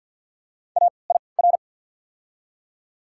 Morse code, heard as a keyed tone, about 48 wpm, spaced as for 28 wpm, 690 Hz. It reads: AIR
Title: 3x Character Spacing - Top 500 Words - 50wpm